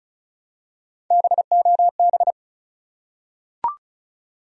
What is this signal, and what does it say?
Morse code, heard as a keyed tone, about 35 wpm, 700 Hz. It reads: BOB
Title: USA Common Names - Max 4 Letters - 35wpm